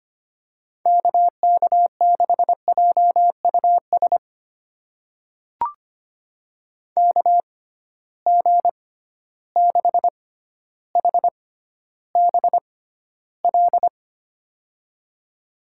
Morse code, heard as a keyed tone, about 25 wpm, spaced as for 13 wpm, 700 Hz. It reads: KK6JUS KG6HBL
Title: USA Call Signs - 6x Character Spacing - 25wpm